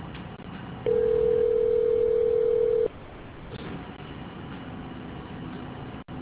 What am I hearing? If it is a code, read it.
Ringback tone (North America), 440 + 480 Hz.